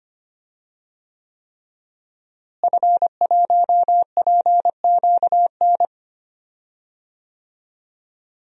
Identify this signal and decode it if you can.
Morse code, heard as a keyed tone, about 25 wpm, 700 Hz. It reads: F1PQN
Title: DXCC-100 Call Signs - 5 Characters - Repeated 3x - 25wpm